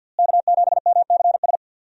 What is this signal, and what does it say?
Morse code, heard as a keyed tone, about 49 wpm, 700 Hz. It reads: K7MCS
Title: USA Call Signs - 5 Characters - Repeated 3x - 50wpm